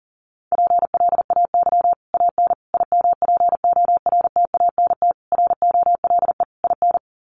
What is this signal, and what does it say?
Morse code, heard as a keyed tone, about 40 wpm, 700 Hz. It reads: PLAY AN IMPORTANT ROLE IN
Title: Sets of 5 Words Encore 40wpm